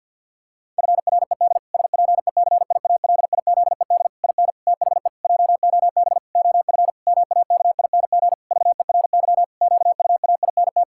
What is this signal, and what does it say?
Morse code, heard as a keyed tone, about 50 wpm, 700 Hz. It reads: FRED SPECIALIZED IN THE JOB OF MAKING VERY QUAINT
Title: Pangram Challenge v2 - 50wpm